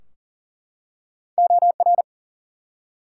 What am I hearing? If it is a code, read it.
Morse code, heard as a keyed tone, about 40 wpm, 700 Hz. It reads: OR
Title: Two Letter Words - 40wpm